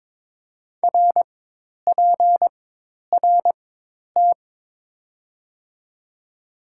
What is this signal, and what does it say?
Morse code, heard as a keyed tone, about 22 wpm, spaced as for 10 wpm, 700 Hz. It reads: RPRT
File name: Top 90 QSO Elements - 4x Character Spacing - 22wpm